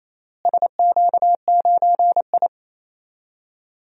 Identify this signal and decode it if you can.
Morse code, heard as a keyed tone, about 28 wpm, 700 Hz. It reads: SQ9I
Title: DXCC-100 Call Signs - 4 Characters - Repeated 3x - 28wpm